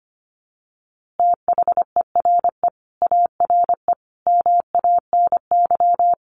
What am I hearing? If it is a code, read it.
Morse code, heard as a keyed tone, about 25 wpm, 700 Hz. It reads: THERE ARE MANY
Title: Easy Phrases 25wpm